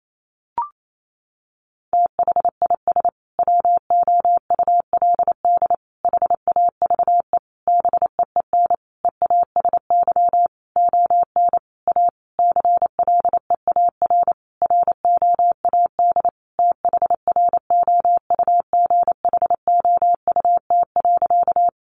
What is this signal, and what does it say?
Morse code, heard as a keyed tone, about 28 wpm, 700 Hz. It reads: THIS WOULD HAVE BEEN EASY ON A CLEAR ROAD THROUGHOUT.
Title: Sentences from Top 800 Words 28wpm